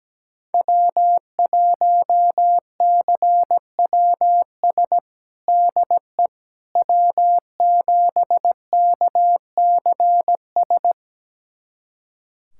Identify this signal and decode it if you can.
Morse code, heard as a keyed tone, about 17 wpm, 700 Hz. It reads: W1CWS DE W7KCS
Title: USA Callsign DE Callsign - 5 Characters - 17wpm